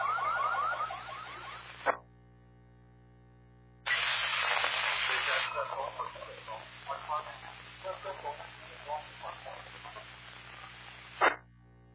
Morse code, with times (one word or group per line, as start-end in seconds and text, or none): none